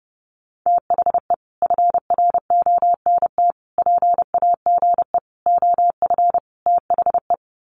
Morse code, heard as a keyed tone, about 30 wpm, 700 Hz. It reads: THE FRONT PAGE OF THE
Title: Sets of 5 Words Encore 30wpm